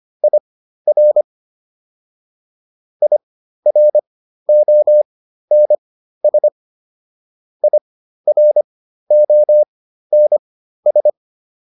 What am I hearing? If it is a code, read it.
Morse code, heard as a keyed tone, about 25 wpm, spaced as for 13 wpm, 600 Hz. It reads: IR IRONS IRONS
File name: IR5 13